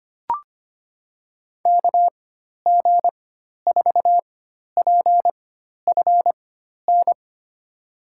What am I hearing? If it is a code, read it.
Morse code, heard as a keyed tone, about 25 wpm, spaced as for 12 wpm, 700 Hz. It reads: KG4PFN
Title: USA Call Signs - 6 Characters - 4x Character Spacing - 25wpm